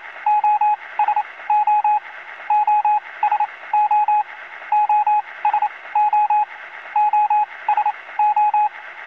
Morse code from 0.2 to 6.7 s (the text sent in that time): OSO OSO OSO